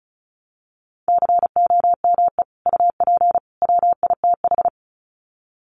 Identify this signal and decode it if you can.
Morse code, heard as a keyed tone, about 34 wpm, 700 Hz. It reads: COME UP WITH
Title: Sets of 3 Words 35wpm